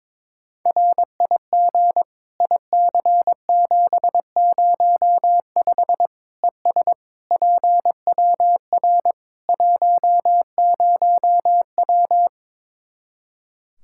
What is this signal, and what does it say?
Morse code, heard as a keyed tone, about 22 wpm, 700 Hz. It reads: RIG IC705 ES PWR 10W
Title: Practice QSO - Exchange 2 - Info - 22wpm